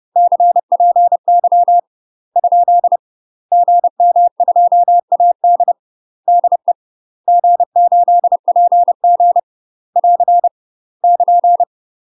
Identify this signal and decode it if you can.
Morse code, heard as a keyed tone, about 30 wpm, 700 Hz. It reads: CPY ? GM2AD DE G8PG <AR> <KN>
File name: UK MASTER QSOs TRAINING / 30WPM / UK MASTER QSOs 0000